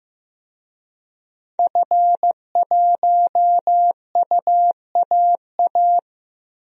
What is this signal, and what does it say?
Morse code, heard as a keyed tone, about 15 wpm, 700 Hz. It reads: F1UAA
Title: DXCC-100 Call Signs - 5 Characters - Repeated 3x - 15wpm